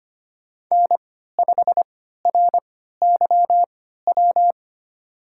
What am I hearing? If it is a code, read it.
Morse code, heard as a keyed tone, about 25 wpm, spaced as for 14 wpm, 700 Hz. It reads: N5RYW
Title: USA Call Signs - 5 Characters - 3x Character Spacing - 25wpm